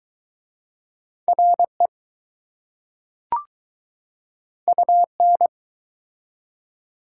Morse code, heard as a keyed tone, about 22 wpm, 700 Hz. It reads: RE UN